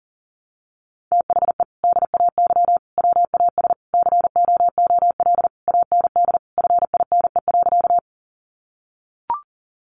Morse code, heard as a keyed tone, about 40 wpm, 700 Hz. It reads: THE DAY WAS COOL AND FINE.
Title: Sentences from Top 1200 Words 40wpm